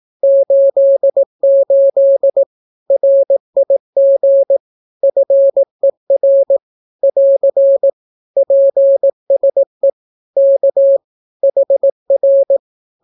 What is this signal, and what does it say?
Morse code, heard as a keyed tone, about 18 wpm, 550 Hz. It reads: 88 RIG FER <AR> PSE K HR
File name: naukapodstaw / skroty / skroty